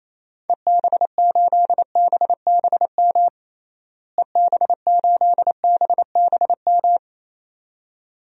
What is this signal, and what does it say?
Morse code, heard as a keyed tone, about 28 wpm, 700 Hz. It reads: EB8BBM EB8BBM
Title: DXCC-100 Call Signs - 6 Characters - Repeated 2x - 28wpm